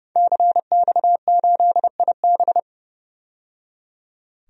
Morse code, heard as a keyed tone, about 30 wpm, 700 Hz. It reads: CX8IB
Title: DXCC-100 Call Signs - 5 Characters - 30wpm